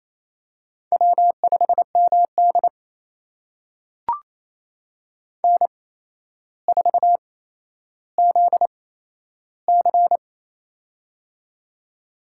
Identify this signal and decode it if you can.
Morse code, heard as a keyed tone, about 28 wpm, spaced as for 12 wpm, 700 Hz. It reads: W5MD N4ZC
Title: USA Call Signs - 8x Character Spacing - 28wpm